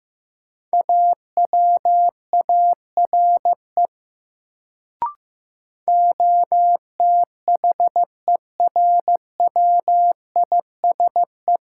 Morse code, heard as a keyed tone, about 15 wpm, 700 Hz. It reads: AWARE OTHERWISE
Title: Sentences from Top 1200 Words - New Word Drill - 15wpm